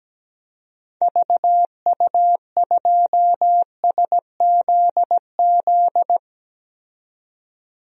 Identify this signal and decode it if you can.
Morse code, heard as a keyed tone, about 17 wpm, 700 Hz. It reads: VU2SZZ
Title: DXCC-100 Call Signs - 6 Characters - 17wpm